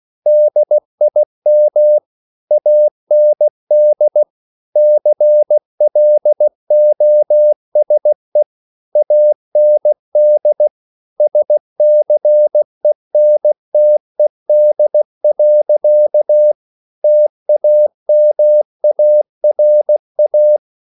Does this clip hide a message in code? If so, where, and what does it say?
Yes: Morse code from 0.3 to 20.6 s, heard as a keyed tone, about 16 wpm, 600 Hz. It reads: DIM AND CLOSE AND SCENTED. TAMARA